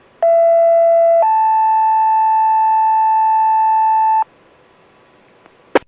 Two-tone page, heard as a one-second tone, then a three-second tone, about 670 then 880 Hz, with noise. It is heard from 0.2 to 4.2 s.